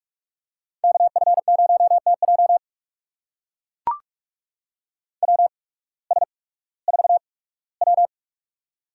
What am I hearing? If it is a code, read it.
Morse code, heard as a keyed tone, about 44 wpm, spaced as for 21 wpm, 700 Hz. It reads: KF0TJ WS4W
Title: USA Call Signs - 8x Character Spacing - 45wpm